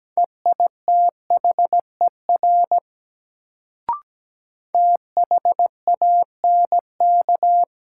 Morse code, heard as a keyed tone, about 17 wpm, 700 Hz. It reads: EITHER THANK
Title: Sentences from Top 600 Words - New Word Drill - 17wpm